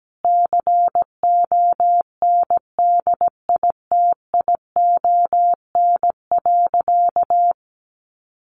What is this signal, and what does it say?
Morse code, heard as a keyed tone, about 17 wpm, 700 Hz. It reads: CONDITION.
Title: Sentences from Top 1200 Words 17wpm